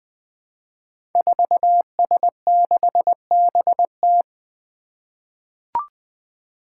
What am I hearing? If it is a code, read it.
Morse code, heard as a keyed tone, about 20 wpm, 700 Hz. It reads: 4S6BT
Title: Five Challenging Letter-Number Combinations 20wpm